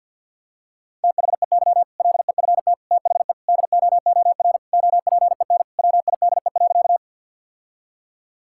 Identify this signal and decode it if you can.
Morse code, heard as a keyed tone, about 48 wpm, 700 Hz. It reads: THEY LEFT THE DOOR OPEN WIDE.
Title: Sentences from Top 1000 Words - Review Part 6 - 50wpm